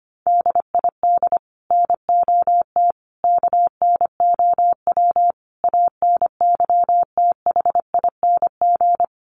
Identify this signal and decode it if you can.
Morse code, heard as a keyed tone, about 25 wpm, 700 Hz. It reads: DID NOT KNOW ANYTHING
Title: Sets of 5 Words Encore2 25wpm